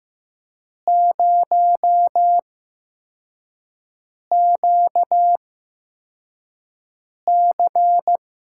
Morse code, heard as a keyed tone, about 15 wpm, spaced as for 4 wpm, 700 Hz. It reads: 0QC
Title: USA Call Signs - 6 Characters - 8x Character Spacing - 15wpm